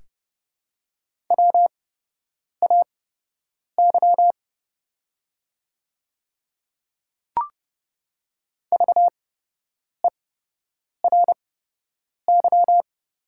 Morse code, heard as a keyed tone, about 31 wpm, spaced as for 8 wpm, 700 Hz. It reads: WAY VERY
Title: 8x Character Spacing - Top 300 Words - 30wpm